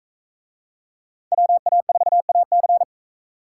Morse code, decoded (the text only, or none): WA4AC